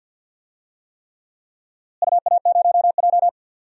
Morse code, heard as a keyed tone, about 51 wpm, 700 Hz. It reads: UA0J
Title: DXCC-100 Call Signs - 4 Characters - Repeated 3x - 50wpm